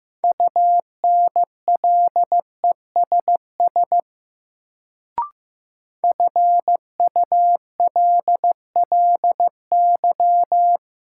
Morse code, heard as a keyed tone, about 15 wpm, 700 Hz. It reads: UNLESS FULLY
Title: Sentences from Top 1300 Words - New Word Drill Alt - 15wpm